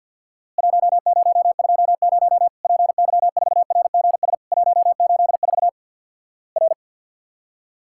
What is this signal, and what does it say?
Morse code, heard as a keyed tone, about 50 wpm, 700 Hz. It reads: 1020 PY4RGS 184 R
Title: Worked All Europe DX Contest - QTC Exchange with Repeats - 50wpm